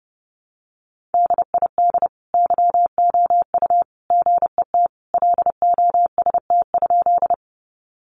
Morse code, heard as a keyed tone, about 30 wpm, 700 Hz. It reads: DID YOU GET LOST?
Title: Sentences from Top 1400 Words 30wpm